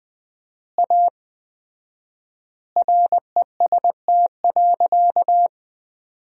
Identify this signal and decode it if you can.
Morse code, heard as a keyed tone, about 20 wpm, 700 Hz. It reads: A REST.
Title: 4x Word-Spacing - Sentences from Top 500 Words - 20wpm